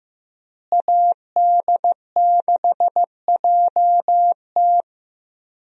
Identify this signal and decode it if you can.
Morse code, heard as a keyed tone, about 16 wpm, 700 Hz. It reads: AD6JT
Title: USA Call Signs - 5 Characters - Repeated 2x - 15wpm